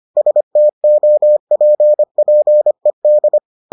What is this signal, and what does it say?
Morse code, heard as a keyed tone, about 25 wpm, 600 Hz. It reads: STOPPED